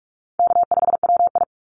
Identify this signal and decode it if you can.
Morse code, heard as a keyed tone, about 46 wpm, 690 Hz. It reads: K5WI
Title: NA SOTA Chaser Callsigns 45wpm